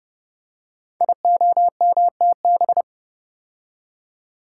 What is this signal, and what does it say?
Morse code, heard as a keyed tone, about 30 wpm, 700 Hz. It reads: IOMTB
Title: Code Groups - 5 Letters - 30wpm